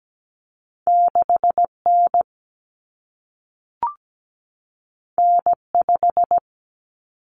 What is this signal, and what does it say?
Morse code, heard as a keyed tone, about 17 wpm, 700 Hz. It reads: 6N N5
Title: Common Cut Numbers - 2-Digit 17wpm